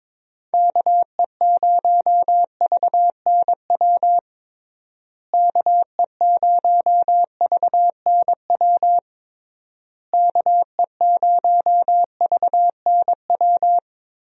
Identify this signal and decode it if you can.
Morse code, heard as a keyed tone, about 22 wpm, 700 Hz. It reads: KE0VNW KE0VNW KE0VNW
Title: USA Call Signs - 6 Characters - Repeated 3x - 22wpm